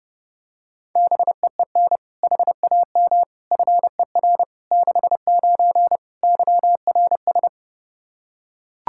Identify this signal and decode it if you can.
Morse code, heard as a keyed tone, about 32 wpm, 700 Hz. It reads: BEEN HAM FER 69 YRS
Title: Practice QSO - Exchange 3 - More Info - 30wpm